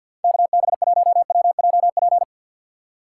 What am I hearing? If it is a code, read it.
Morse code, heard as a keyed tone, about 53 wpm, 700 Hz. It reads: KB1WJP
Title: USA Call Signs - 3x Character Spacing - 50wpm